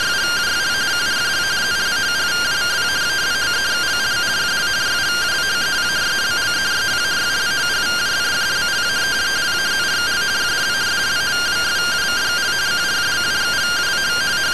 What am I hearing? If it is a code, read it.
Two-tone teleprinter signal, about 170 Hz shift.